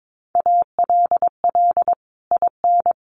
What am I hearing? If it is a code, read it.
Morse code, heard as a keyed tone, about 22 wpm, 700 Hz. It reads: ALL IN
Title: Sentences from Top 1500 Words 22wpm